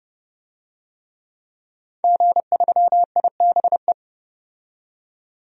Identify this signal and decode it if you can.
Morse code, heard as a keyed tone, about 30 wpm, 700 Hz. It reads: G3IBE
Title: DXCC-100 Call Signs - 5 Characters - 30wpm